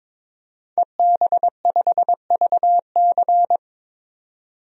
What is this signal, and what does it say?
Morse code, heard as a keyed tone, about 22 wpm, 700 Hz. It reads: EB5VC
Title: DXCC-100 Call Signs - 5 Characters - 22wpm